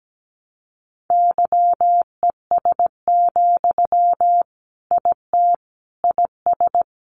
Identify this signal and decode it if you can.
Morse code, heard as a keyed tone, about 17 wpm, 700 Hz. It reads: YES, IT IS
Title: Sentences from Top 500 Words 17wpm